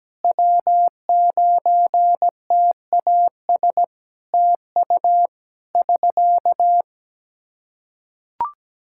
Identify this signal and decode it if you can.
Morse code, heard as a keyed tone, about 17 wpm, 700 Hz. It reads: W9TAS TU <SK>
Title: Practice QSO - Exchange 4 - Ending - 17wpm